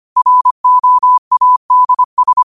Morse code, heard as a keyed tone, about 25 wpm, 1000 Hz. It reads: ROADS
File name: qotdfiles / 25wpm